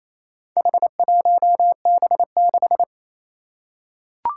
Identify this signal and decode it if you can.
Morse code, heard as a keyed tone, about 27 wpm, 700 Hz. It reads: H1B6
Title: Four Challenging Letter-Number Combinations 28wpm